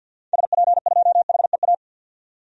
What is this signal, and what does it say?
Morse code, heard as a keyed tone, about 49 wpm, 700 Hz. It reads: SP2HEA
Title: DXCC-100 Call Signs - 6 Characters - 50wpm